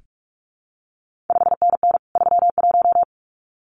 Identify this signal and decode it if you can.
Morse code, heard as a keyed tone, about 46 wpm, 700 Hz. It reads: 5NN 31
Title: CQ WW DX Contest Exchange Encore - 45wpm